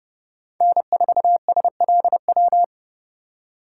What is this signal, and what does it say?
Morse code, heard as a keyed tone, about 30 wpm, 700 Hz. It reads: N4SLW
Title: USA Call Signs - 2x Character Spacing - 30wpm